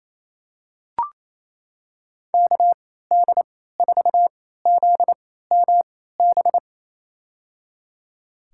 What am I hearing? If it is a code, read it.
Morse code, heard as a keyed tone, about 28 wpm, spaced as for 16 wpm, 700 Hz. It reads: KD4ZMB
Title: USA Call Signs - 3x Character Spacing - 28wpm